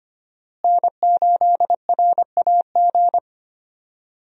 Morse code, heard as a keyed tone, about 25 wpm, 700 Hz. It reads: N8RAG
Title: USA Call Signs - 5 Characters - Repeated 2x - 25wpm